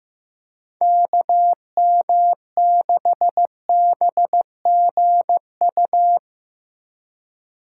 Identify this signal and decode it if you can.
Morse code, heard as a keyed tone, about 15 wpm, 700 Hz. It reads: KM6BGU
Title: USA Call Signs - 6 Characters - 3x Character Spacing - 15wpm